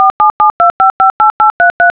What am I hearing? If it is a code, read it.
DTMF, heard as telephone touch tones, tones of about 130 ms. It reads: 4772558833